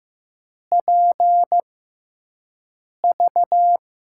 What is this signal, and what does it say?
Morse code, heard as a keyed tone, about 15 wpm, spaced as for 5 wpm, 700 Hz. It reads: PV
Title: USA Call Signs - 6 Characters - 6x Character Spacing - 15wpm